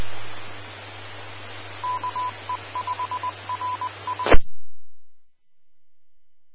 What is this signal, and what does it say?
Morse code, heard as a keyed tone, about 20 wpm, 1000 Hz. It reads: KE5RI
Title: Austin, TX Amateur Repeaters